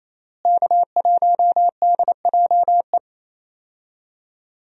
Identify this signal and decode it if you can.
Morse code, heard as a keyed tone, about 28 wpm, 700 Hz. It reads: K1DJE
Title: USA Call Signs - 5 Characters - Repeated 3x - 28wpm